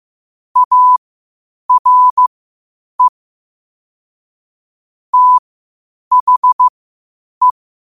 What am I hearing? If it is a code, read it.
Morse code, heard as a keyed tone, about 15 wpm, spaced as for 8 wpm, 1000 Hz. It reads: ARE THE